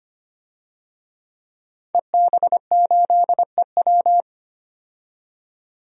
Morse code, heard as a keyed tone, about 25 wpm, 700 Hz. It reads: EB8EW
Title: DXCC-100 Call Signs - 5 Characters - 25wpm